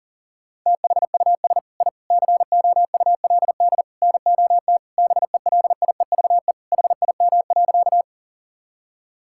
Morse code, heard as a keyed tone, about 40 wpm, 700 Hz. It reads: THUS I COULD NOT BELIEVE HIM.
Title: Sentences from Top 1000 Words - Review Part 5 - 40wpm